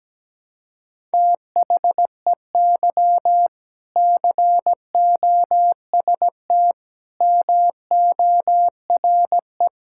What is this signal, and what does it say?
Morse code, heard as a keyed tone, about 17 wpm, 700 Hz. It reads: THEY COST MORE
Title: Sentences from Top 1000 Words - Review Part 5 - 17wpm